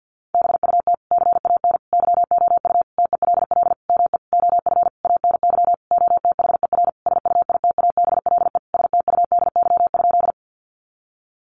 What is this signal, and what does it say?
Morse code, heard as a keyed tone, about 50 wpm, 700 Hz. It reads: BUT CAN YOU TELL ME OF ANY OTHER SUITABLE STUDY?
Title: Sentences from Top 1400 Words 50wpm